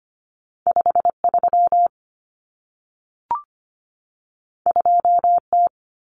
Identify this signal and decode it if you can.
Morse code, heard as a keyed tone, about 25 wpm, 700 Hz. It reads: 53 2T
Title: Common Cut Numbers - 2-Digit 25wpm